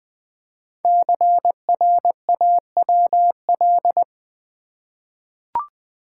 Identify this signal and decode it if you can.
Morse code, heard as a keyed tone, about 20 wpm, 700 Hz. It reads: CRAWL
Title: 5 Letter Words - First 3 Letters Given - 20wpm